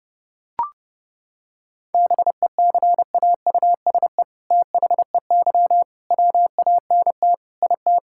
Morse code, heard as a keyed tone, about 30 wpm, 700 Hz. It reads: BECAUSE THEY WANT IT
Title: Sentences from Top 1000 Words - Review Part 5 - 30wpm